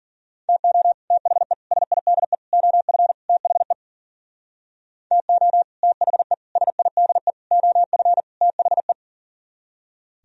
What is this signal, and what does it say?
Morse code, heard as a keyed tone, about 50 wpm, 700 Hz. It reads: TO THE SIDE OF THE TO THE SIDE OF THE
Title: Speed-Racing - Sets of 5 Words 40wpm